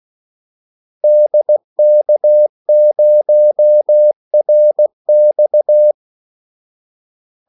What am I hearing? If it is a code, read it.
Morse code, heard as a keyed tone, about 16 wpm, 600 Hz. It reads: DK0RX
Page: Kaldesignaler 16 wpm